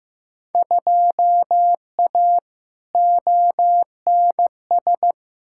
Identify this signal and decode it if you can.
Morse code, heard as a keyed tone, about 15 wpm, 700 Hz. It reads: 2A ONS
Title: Field Day Exchanges Encore - 15wpm